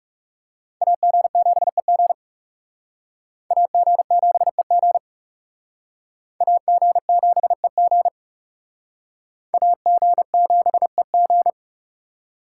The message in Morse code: AG7EG AG7EG AG7EG AG7EG

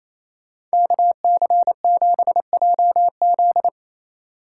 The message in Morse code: KC7JZ

